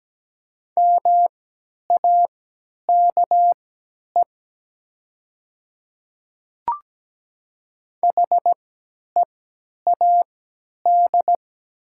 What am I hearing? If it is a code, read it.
Morse code, heard as a keyed tone, about 17 wpm, spaced as for 10 wpm, 700 Hz. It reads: MAKE HEAD
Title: 3x Character Spacing - Top 500 Words - 17wpm